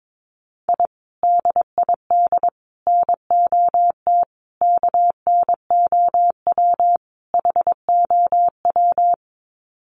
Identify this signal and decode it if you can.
Morse code, heard as a keyed tone, about 22 wpm, 700 Hz. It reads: I DID NOT KNOW HOW
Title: Sets of 5 Words Encore 22wpm